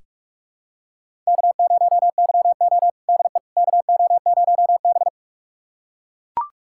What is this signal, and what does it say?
Morse code, heard as a keyed tone, about 41 wpm, 700 Hz. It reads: K0YO DE KO0B
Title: USA Callsign DE Callsign - 4 Characters - 45wpm